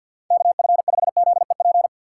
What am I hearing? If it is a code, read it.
Morse code, heard as a keyed tone, about 46 wpm, 700 Hz. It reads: KF5ZEP
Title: USA Call Signs - 6 Characters - 4x Character Spacing - 50wpm